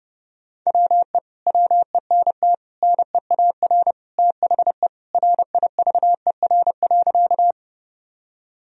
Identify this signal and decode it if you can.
Morse code, heard as a keyed tone, about 32 wpm, 700 Hz. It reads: WE WENT NEAR THE RIVER.
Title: Sentences from Top 1000 Words - Review Part 1 - 30wpm